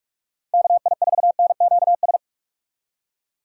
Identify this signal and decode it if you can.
Morse code, heard as a keyed tone, about 41 wpm, 700 Hz. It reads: KI4NQS
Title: USA Call Signs - 6 Characters - 2x Character Spacing - 45wpm